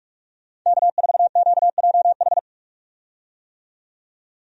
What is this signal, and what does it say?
Morse code, heard as a keyed tone, about 46 wpm, 700 Hz. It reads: K4QJH
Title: USA Call Signs - 5 Characters - 45wpm